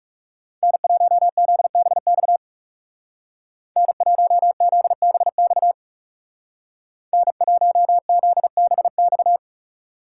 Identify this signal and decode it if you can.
Morse code, heard as a keyed tone, about 37 wpm, 700 Hz. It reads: N1ZBX N1ZBX N1ZBX
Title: Speed Racing - USA 1x3 Call Signs 30wpm